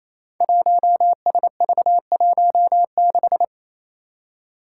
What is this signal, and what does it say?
Morse code, heard as a keyed tone, about 28 wpm, 700 Hz. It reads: 1SV16
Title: Five Challenging Letter-Number Combinations 28wpm